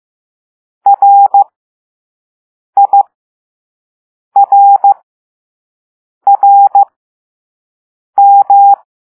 Morse code, heard as a keyed tone, about 15 wpm, spaced as for 5 wpm, 800 Hz. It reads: RIRRM
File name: Morse / Level2B